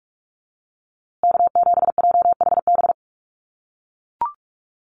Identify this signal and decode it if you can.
Morse code, heard as a keyed tone, about 44 wpm, 700 Hz. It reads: K7JHB
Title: USA Call Signs Repeated 2x 45wpm